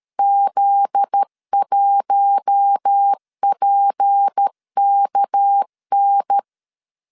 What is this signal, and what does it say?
Morse code, heard as a keyed tone, about 13 wpm, 800 Hz. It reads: Z1PKN
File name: LEZIONI / 9 2 12wpm 2301